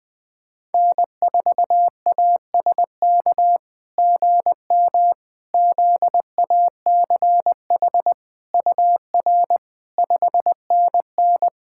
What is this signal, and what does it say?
Morse code, heard as a keyed tone, about 20 wpm, 700 Hz. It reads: N4ASK GM ZACH UR 5NN
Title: SOTA QSO Practice 20wpm